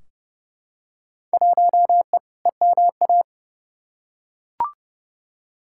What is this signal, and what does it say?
Morse code, heard as a keyed tone, about 30 wpm, 700 Hz. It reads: 1E EMA
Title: Field Day Exchanges Encore - 30wpm